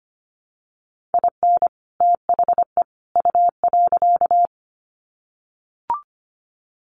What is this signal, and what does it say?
Morse code, heard as a keyed tone, about 25 wpm, 700 Hz. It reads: IN THE U.
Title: Sets of 3 Words 25wpm